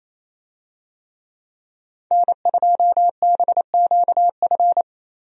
Morse code, heard as a keyed tone, about 27 wpm, 700 Hz. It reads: N2BQF
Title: USA Call Signs - 5 Characters - Repeated 3x - 28wpm